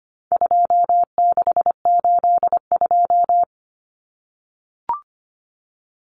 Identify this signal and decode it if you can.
Morse code, heard as a keyed tone, about 25 wpm, 700 Hz. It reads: 2682
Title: Common Cut Numbers - 4-Digit 25wpm